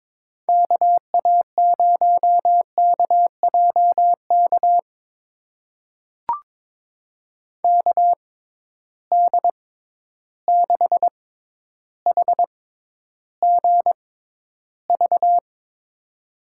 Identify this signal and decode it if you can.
Morse code, heard as a keyed tone, about 22 wpm, spaced as for 11 wpm, 700 Hz. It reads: KA0KJK KD6HGV
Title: USA Call Signs - 6 Characters - 6x Character Spacing - 22wpm